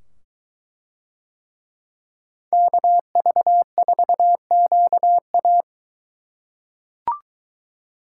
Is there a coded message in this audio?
Yes: Morse code from 2.5 to 5.6 s, heard as a keyed tone, about 22 wpm, 700 Hz. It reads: KV4QA